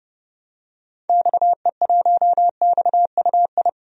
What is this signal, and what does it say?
Morse code, heard as a keyed tone, about 32 wpm, 700 Hz. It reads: XE1XUI
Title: DXCC-100 Call Signs - 6 Characters - 30wpm